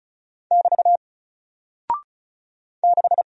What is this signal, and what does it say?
Morse code, heard as a keyed tone, about 36 wpm, 700 Hz. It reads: <BT> 6
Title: Single Character (Letter:Number:Punctuation:Prosign) Alt 35wpm